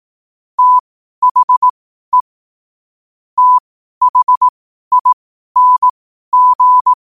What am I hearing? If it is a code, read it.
Morse code, heard as a keyed tone, about 18 wpm, spaced as for 12 wpm, 1000 Hz. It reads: THE THING
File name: qotdfiles / 12wpm